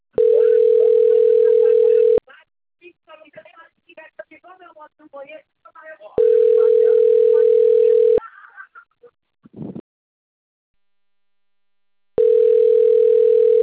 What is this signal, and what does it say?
Ringback tone (North America), 440 + 480 Hz, with speech.
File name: rg-1600-unknown- - - .43061